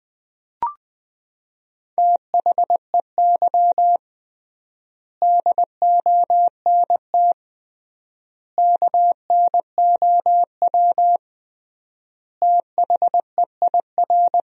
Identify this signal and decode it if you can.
Morse code, heard as a keyed tone, about 21 wpm, 700 Hz. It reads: THEY DONT KNOW THEIR
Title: 3x Word-Spacing - Sentences from Top 300 Words - 20wpm